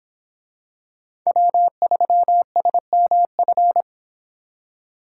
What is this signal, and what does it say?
Morse code, heard as a keyed tone, about 25 wpm, 700 Hz. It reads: W3SMF